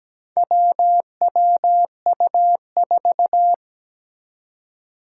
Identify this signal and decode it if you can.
Morse code, heard as a keyed tone, about 17 wpm, 700 Hz. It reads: WWU4
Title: Four Challenging Letter-Number Combinations 17wpm